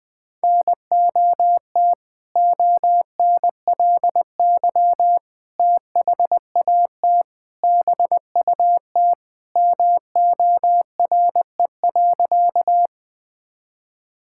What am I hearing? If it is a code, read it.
Morse code, heard as a keyed tone, about 20 wpm, 700 Hz. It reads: NOT ONLY THAT BUT MORE.
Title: Sentences with 2-Word Phrases - 20wpm